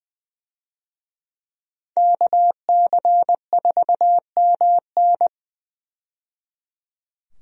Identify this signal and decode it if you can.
Morse code, heard as a keyed tone, about 20 wpm, 700 Hz. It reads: KC4MN
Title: USA Call Signs - 5 Characters - 20wpm